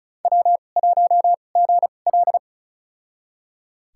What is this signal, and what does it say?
Morse code, heard as a keyed tone, about 36 wpm, spaced as for 25 wpm, 690 Hz. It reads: W1GL
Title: USA Call Signs - 4 Characters - 2x Character Spacing - 35wpm